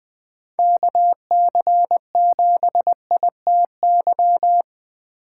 Morse code, heard as a keyed tone, about 24 wpm, 700 Hz. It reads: KC7ITY
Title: Speed Racing - USA 2x3 Call Signs 20wpm